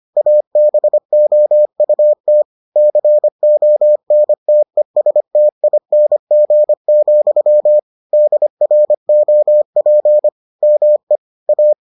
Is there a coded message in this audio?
Yes: Morse code from 0.2 to 11.7 s, heard as a keyed tone, about 25 wpm, 600 Hz. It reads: ABOUT CONTESTING, DROP ME A